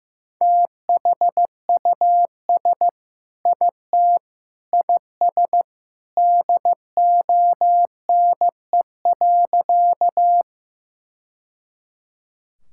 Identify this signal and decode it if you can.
Morse code, heard as a keyed tone, about 15 wpm, 700 Hz. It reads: THUS IT IS DONE.
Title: Sentences from Top 1000 Words - Review Part 5 - 15wpm